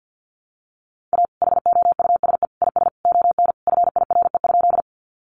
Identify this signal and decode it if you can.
Morse code, heard as a keyed tone, about 50 wpm, 700 Hz. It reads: A HOUSE IS ON FIRE?
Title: Sentences from Top 400 Words 50wpm